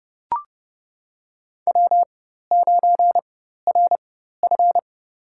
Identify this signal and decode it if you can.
Morse code, heard as a keyed tone, about 30 wpm, spaced as for 14 wpm, 700 Hz. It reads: W9RF
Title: USA Call Signs - 4 Characters - 4x Character Spacing - 30wpm